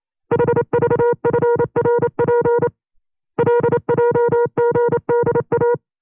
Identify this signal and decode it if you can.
Morse code, heard as a keyed tone, about 28 wpm, 480 Hz. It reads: HVFRP LJGDA
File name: FreeFiles / CW-practice / 25WPM-side1